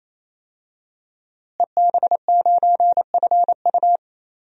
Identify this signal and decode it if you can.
Morse code, heard as a keyed tone, about 28 wpm, 700 Hz. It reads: EB9FU
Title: DXCC-100 Call Signs - 5 Characters - Repeated 3x - 28wpm